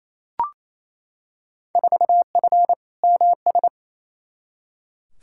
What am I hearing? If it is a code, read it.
Morse code, heard as a keyed tone, about 28 wpm, 700 Hz. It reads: 4F MS
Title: Field Day Exchanges Encore - 28wpm